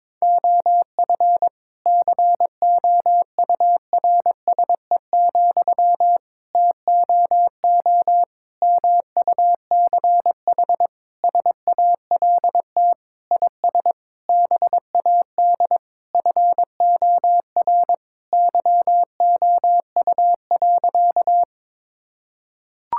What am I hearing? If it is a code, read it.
Morse code, heard as a keyed tone, about 22 wpm, 700 Hz. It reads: OF COURSE, TOO MUCH SALT IS BAD FOR YOU.
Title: Sentences from Top 1000 Words - Review Part 5 - 22wpm